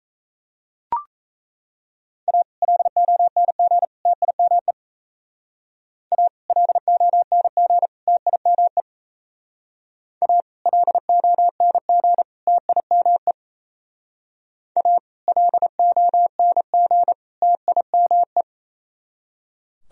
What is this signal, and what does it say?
Morse code, heard as a keyed tone, about 36 wpm, 700 Hz. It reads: A LONG TIME A LONG TIME A LONG TIME A LONG TIME